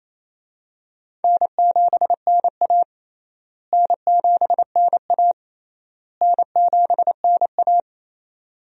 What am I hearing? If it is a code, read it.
Morse code, heard as a keyed tone, about 28 wpm, 700 Hz. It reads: N7NA N7NA N7NA
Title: USA Call Signs - 4 Characters - Repeated 3x - 28wpm